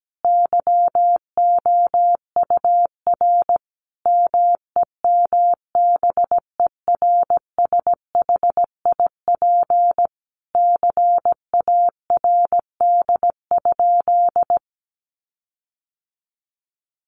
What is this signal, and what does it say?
Morse code, heard as a keyed tone, about 17 wpm, 700 Hz. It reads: YOUR MEMBERSHIP CARD?
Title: Sentences from Top 1400 Words 17wpm